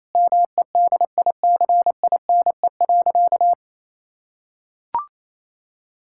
Morse code, heard as a keyed tone, about 28 wpm, 700 Hz. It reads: MEDICINE.
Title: Sentences from Top 1000 Words - Review Part 6 - 28wpm